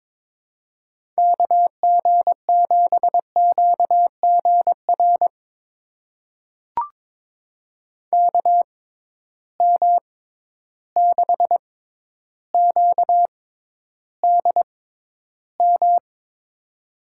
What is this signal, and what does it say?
Morse code, heard as a keyed tone, about 22 wpm, spaced as for 11 wpm, 700 Hz. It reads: KG7QGR KM6QDM
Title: USA Call Signs - 6 Characters - 6x Character Spacing - 22wpm